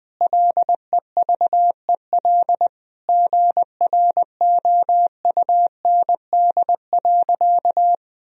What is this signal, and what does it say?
Morse code, heard as a keyed tone, about 19 wpm, 700 Hz. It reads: LEVEL GROUND.
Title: Sentences from Top 1000 Words - Review Part 4 - 20wpm